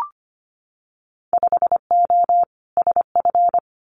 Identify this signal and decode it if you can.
Morse code, heard as a keyed tone, about 25 wpm, 700 Hz. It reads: 5O SF
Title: Winter Field Day Exchanges Encore 25wpm